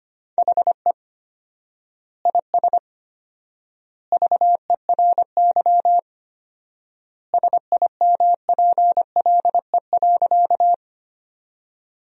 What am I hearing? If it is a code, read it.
Morse code, heard as a keyed tone, about 25 wpm, 700 Hz. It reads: HE IS VERY SIMPLE.
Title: 4x Word-Spacing - Sentences from Top 400 Words - 25wpm